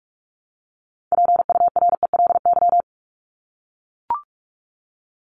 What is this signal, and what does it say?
Morse code, heard as a keyed tone, about 44 wpm, 700 Hz. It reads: PURELY
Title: LY Suffix - Top 100 Words with Base Words 45wpm